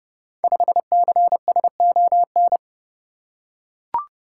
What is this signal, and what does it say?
Morse code, heard as a keyed tone, about 30 wpm, 700 Hz. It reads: 5CSON
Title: Code Groups - 5 Letter:Numbers - 30wpm